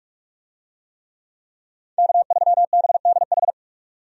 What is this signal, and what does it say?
Morse code, heard as a keyed tone, about 46 wpm, 700 Hz. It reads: K3BDH
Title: USA Call Signs - 5 Characters - 45wpm